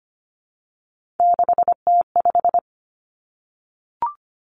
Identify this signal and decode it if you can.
Morse code, heard as a keyed tone, about 25 wpm, 700 Hz. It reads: 6T5
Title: Three Challenging Letter-Number Combinations 25wpm